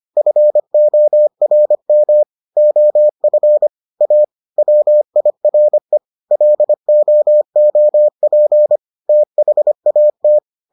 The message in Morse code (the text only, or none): FORM OF A WIRE LOOP THAT